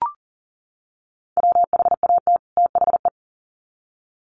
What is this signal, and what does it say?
Morse code, heard as a keyed tone, about 40 wpm, 700 Hz. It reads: WHAT THE
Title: Sets of 2 Words 40wpm